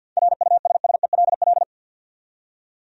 Morse code, heard as a keyed tone, about 53 wpm, 700 Hz. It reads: RUSSELL
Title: USA Common Names - 50wpm